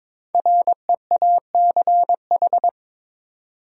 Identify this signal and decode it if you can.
Morse code, heard as a keyed tone, about 22 wpm, 700 Hz. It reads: REACH
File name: 5 Letter Words - First 3 Letters Given - 22wpm